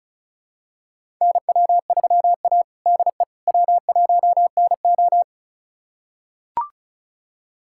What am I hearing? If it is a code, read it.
Morse code, heard as a keyed tone, about 36 wpm, 700 Hz. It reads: NW3A DE W1NO
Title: USA Callsign DE Callsign - 4 Characters - 35wpm